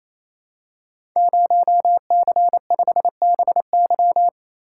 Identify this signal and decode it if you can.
Morse code, heard as a keyed tone, about 28 wpm, 700 Hz. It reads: 0C5BY
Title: Code Groups - 5 Letter:Numbers - 28wpm